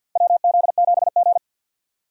Morse code, heard as a keyed tone, about 52 wpm, 700 Hz. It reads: WZ7G
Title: USA Call Signs - 4 Characters - 3x Character Spacing - 50wpm